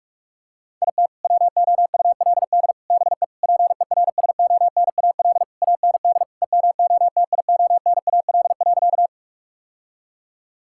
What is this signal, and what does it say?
Morse code, heard as a keyed tone, about 46 wpm, 700 Hz. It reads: IT WOULD BE PERSONAL AND EMOTIONAL.
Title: Sentences from Top 1000 Words - Review Part 5 - 45wpm